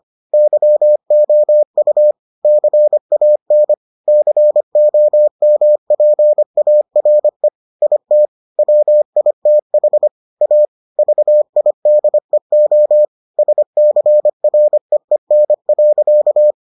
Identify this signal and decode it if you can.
Morse code, heard as a keyed tone, about 25 wpm, 600 Hz. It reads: YOU CAN COMPARE IT WITH A VIDEO SCREEN.